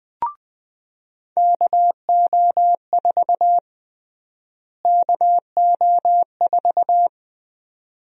Morse code, heard as a keyed tone, about 20 wpm, 700 Hz. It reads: KO4 KO4
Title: USA Call Sign Prefixes - Repeated 2x - 20wpm